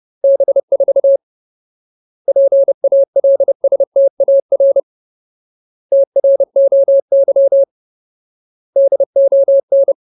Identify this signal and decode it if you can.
Morse code, heard as a keyed tone, about 30 wpm, 550 Hz. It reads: B4 PALSTAR TROY DON
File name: Ultimate Word List / 30WPM Ultimate